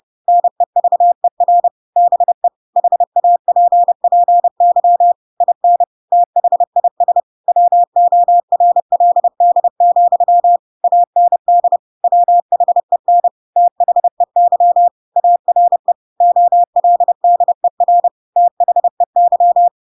Morse code, heard as a keyed tone, about 30 wpm, 700 Hz. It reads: NEVER BE HAPPY IN THIS WORLD, AND WHEN THEY ARE OLDER THEY
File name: Pinocchio / Pinocchio ch4 30 30 ik1hge